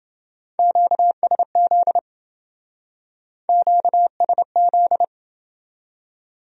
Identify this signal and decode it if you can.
Morse code, heard as a keyed tone, about 28 wpm, 700 Hz. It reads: QSZ QSZ